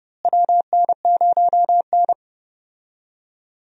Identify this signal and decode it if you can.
Morse code, heard as a keyed tone, about 31 wpm, 700 Hz. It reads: WN0N
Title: USA Call Signs - 4 Characters - Repeated 2x - 30wpm